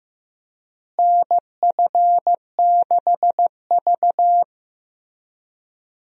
Morse code, heard as a keyed tone, about 15 wpm, 700 Hz. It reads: NF6V
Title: USA Call Signs - 4 Characters - Repeated 3x - 15wpm